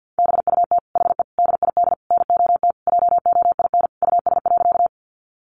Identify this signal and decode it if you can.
Morse code, heard as a keyed tone, about 50 wpm, 700 Hz. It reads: BUT HE DID NOT JOIN US.
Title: Sentences from Top 1400 Words 50wpm